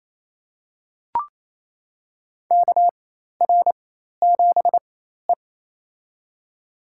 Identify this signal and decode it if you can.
Morse code, heard as a keyed tone, about 28 wpm, spaced as for 13 wpm, 700 Hz. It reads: KR7E
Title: USA Call Signs - 4 Characters - 4x Character Spacing - 28wpm